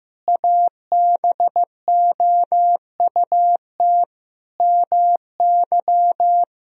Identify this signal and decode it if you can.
Morse code, heard as a keyed tone, about 15 wpm, 700 Hz. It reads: ABOUT MY
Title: Sentences from Top 1000 Words - Review Part 2 - 15wpm